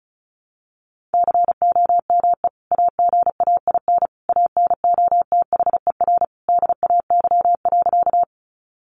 Morse code, heard as a keyed tone, about 34 wpm, 700 Hz. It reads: COME AGAIN ANOTHER DAY.
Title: Sentences from Top 600 Words 35wpm